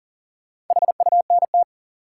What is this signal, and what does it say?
Morse code, heard as a keyed tone, about 40 wpm, 700 Hz. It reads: HUNT
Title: Sentences from Top 1600 Words - New Word Drill - 40wpm